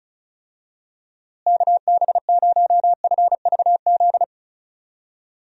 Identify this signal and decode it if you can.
Morse code, heard as a keyed tone, about 36 wpm, 700 Hz. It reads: KB0FVZ
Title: USA Call Signs - 6 Characters - Repeated 3x - 35wpm